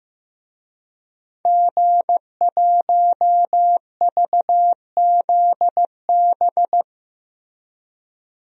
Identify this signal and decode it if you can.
Morse code, heard as a keyed tone, about 15 wpm, 700 Hz. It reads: G1VZB
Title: DXCC-100 Call Signs - 5 Characters - Repeated 3x - 15wpm